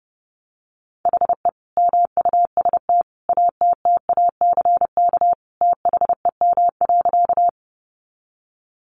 Morse code, heard as a keyed tone, about 30 wpm, 700 Hz. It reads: HE MUST ATTACK THEM.
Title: Sentences from Top 1500 Words 30wpm